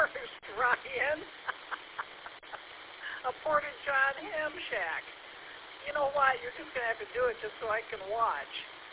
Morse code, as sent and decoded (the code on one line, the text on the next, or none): none